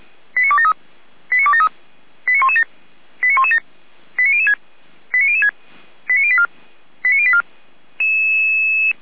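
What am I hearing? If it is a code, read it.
Five-tone selective call (ZVEI), heard as quick runs of five tones, tones of about 70 ms. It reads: 89273 89273 89117 89117 89006 89006 89084 89084
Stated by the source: ZVEI five-tone selective call 89230, received with an SDR.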